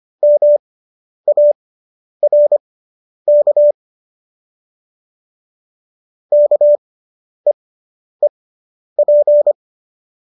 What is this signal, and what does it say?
Morse code, heard as a keyed tone, about 25 wpm, spaced as for 10 wpm, 590 Hz. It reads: MARK KEEP